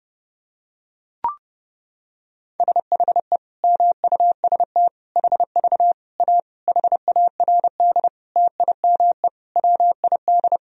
Morse code, heard as a keyed tone, about 30 wpm, 700 Hz. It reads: SHE MUST HV A HARD TIME WID
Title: Sentences from Top 300 Words and Abbreviations 30wpm